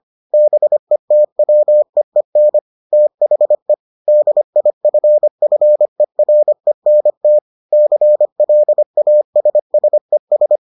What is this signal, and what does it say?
Morse code, heard as a keyed tone, about 25 wpm, 600 Hz. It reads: BETWEEN THE DIFFERENT CLASSES